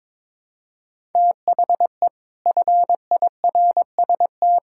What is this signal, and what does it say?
Morse code, heard as a keyed tone, about 21 wpm, 700 Hz. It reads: THE FIRST
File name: practice / 22wpm / Sets of 7 Words - 22wpm